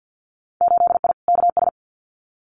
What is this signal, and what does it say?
Morse code, heard as a keyed tone, about 50 wpm, 700 Hz. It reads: 8I KS
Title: Winter Field Day Exchanges Encore 50wpm